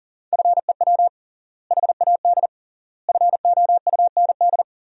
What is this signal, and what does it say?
Morse code, heard as a keyed tone, about 40 wpm, 700 Hz. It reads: FEW HAD FOUND
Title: 3x Word-Spacing - Sentences from Top 300 Words - 40wpm